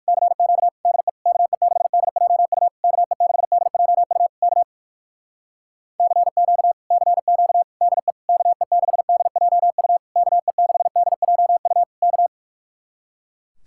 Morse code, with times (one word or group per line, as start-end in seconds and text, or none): 0.1-0.7 CQ
0.9-1.1 DE
1.3-2.7 KE6DJU
2.8-4.3 KE6DJU
4.4-4.6 K
6.0-6.7 CQ
6.9-7.6 CQ
7.8-8.1 DE
8.3-10.0 KE6DJU
10.2-11.8 KE6DJU
12.0-12.3 K